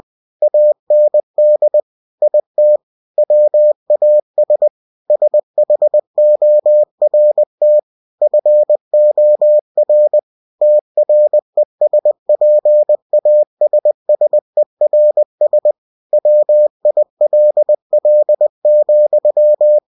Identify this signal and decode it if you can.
Morse code, heard as a keyed tone, about 20 wpm, 600 Hz. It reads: AND IT WAS SHORT FOR TRESPASSERS WILL,
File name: cw-books / winnie-the-pooh / winnie-20wpm-page-0020